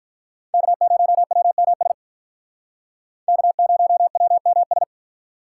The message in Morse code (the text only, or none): K0WMS K0WMS